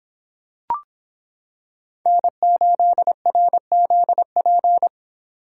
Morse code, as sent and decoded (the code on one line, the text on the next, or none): -. ---.. .-. --.. .--.
N8RZP